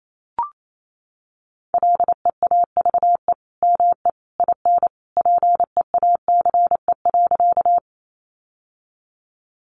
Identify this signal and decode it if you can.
Morse code, heard as a keyed tone, about 28 wpm, 700 Hz. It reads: LEAVE ME IN PEACE.
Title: Sentences from Top 1500 Words 28wpm